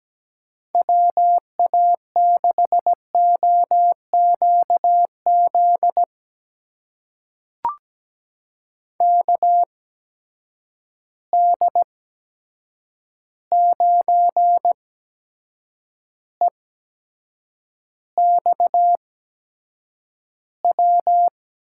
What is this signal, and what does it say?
Morse code, heard as a keyed tone, about 17 wpm, spaced as for 7 wpm, 700 Hz. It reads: WA6OQZ KD9EXW
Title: USA Call Signs - 6 Characters - 8x Character Spacing - 17wpm